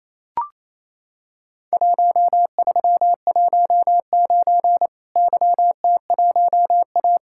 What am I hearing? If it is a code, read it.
Morse code, heard as a keyed tone, about 27 wpm, 700 Hz. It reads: 1319 YT1A
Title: Worked All Europe DX Contest - QTC Exchange - 28wpm